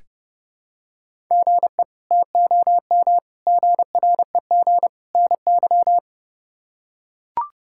Morse code, heard as a keyed tone, about 30 wpm, 700 Hz. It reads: GE TOM GREG NY
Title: K1USN SST QSO Practice - 30wpm